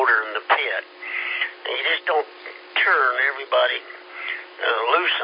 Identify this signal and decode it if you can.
Emergency Alert System (SAME) tones, heard as harsh buzzing bursts, with speech. It is heard from 0.7 to 4.6 s.